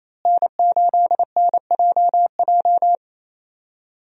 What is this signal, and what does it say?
Morse code, heard as a keyed tone, about 28 wpm, 700 Hz. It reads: N8NJJ
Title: USA Call Signs - 5 Characters - Repeated 2x - 28wpm